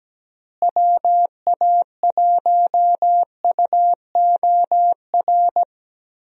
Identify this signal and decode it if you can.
Morse code, heard as a keyed tone, about 17 wpm, 700 Hz. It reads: WA1UOR
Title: USA Call Signs - 6 Characters - Repeated 3x - 17wpm